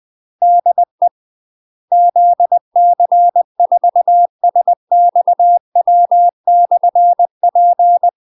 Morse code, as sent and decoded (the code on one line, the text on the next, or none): -.. / --.. -.-. ....- ... -..- .-- -..-. .--.
DE ZC4SXW/P